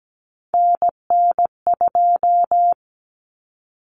Morse code, heard as a keyed tone, about 17 wpm, 700 Hz. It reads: NN2
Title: Common Cut Numbers - 3-Digit 17wpm